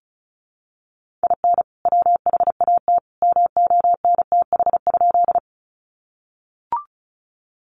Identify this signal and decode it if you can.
Morse code, heard as a keyed tone, about 34 wpm, 700 Hz. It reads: IN WHAT MONTH?
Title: Sentences from Top 800 Words 35wpm